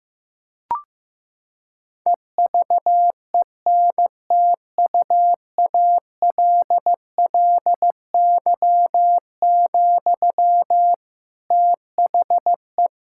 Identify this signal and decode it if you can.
Morse code, heard as a keyed tone, about 15 wpm, 700 Hz. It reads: EVENTUALLY, THE
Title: Sentences from Top 1000 Words - Review Part 6 - 15wpm